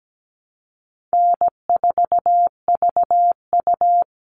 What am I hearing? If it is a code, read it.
Morse code, heard as a keyed tone, about 17 wpm, 700 Hz. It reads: N4VU
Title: All Cut Numbers - 4-Digit 17wpm